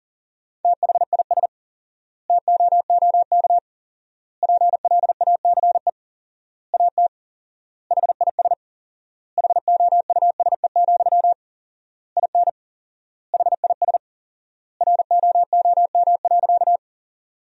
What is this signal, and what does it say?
Morse code, heard as a keyed tone, about 40 wpm, 700 Hz. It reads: THIS TOOK PLACE AT HIS HOUSE, IN HIS ROOM.
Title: 4x Word-Spacing - Sentences from Top 300 Words - 40wpm